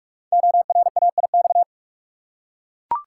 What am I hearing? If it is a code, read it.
Morse code, heard as a keyed tone, about 44 wpm, 700 Hz. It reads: ORAIX
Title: Code Groups - 5 Letters - 45wpm